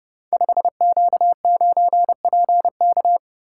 Morse code, heard as a keyed tone, about 30 wpm, 700 Hz. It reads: 5Q9PK
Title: Code Groups - 5 Letter:Numbers - 30wpm